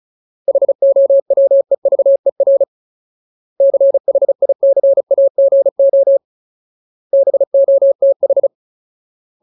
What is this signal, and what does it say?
Morse code, heard as a keyed tone, about 34 wpm, 550 Hz. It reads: HOWEVER CHICAGO BOTH
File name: Ultimate Word List / 35WPM Ultimate